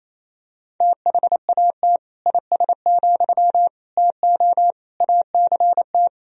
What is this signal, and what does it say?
Morse code, heard as a keyed tone, about 28 wpm, 700 Hz. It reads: THAT IS, TO ACT
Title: Sentences from Top 1000 Words - Review Part 4 - 28wpm